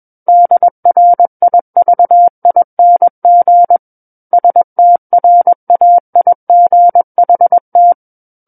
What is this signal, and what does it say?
Morse code, heard as a keyed tone, about 21 wpm, 700 Hz. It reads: DRIVING STRAIGHT
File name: Book One / 12-Chapter 12 - 21 WPM